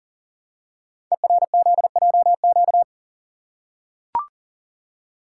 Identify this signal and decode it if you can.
Morse code, heard as a keyed tone, about 40 wpm, 700 Hz. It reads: ERZJQ
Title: Code Groups - 5 Letters - 40wpm